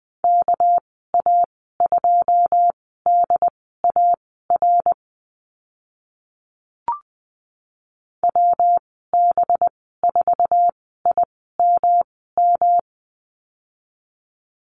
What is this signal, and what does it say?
Morse code, heard as a keyed tone, about 20 wpm, spaced as for 14 wpm, 700 Hz. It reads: KA2DAR WB4IMM
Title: 207 Lesson 33g - Letter j - Call Signs 20wpm